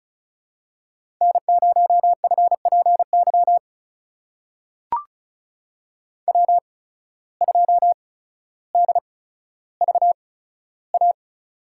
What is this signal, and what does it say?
Morse code, heard as a keyed tone, about 36 wpm, spaced as for 15 wpm, 700 Hz. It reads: N0FPY W2DVA
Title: USA Call Signs - 5 Characters - 8x Character Spacing - 35wpm